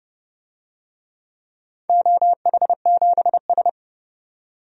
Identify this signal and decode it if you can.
Morse code, heard as a keyed tone, about 32 wpm, 700 Hz. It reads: OH7S
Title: DXCC-100 Call Signs - 4 Characters - 30wpm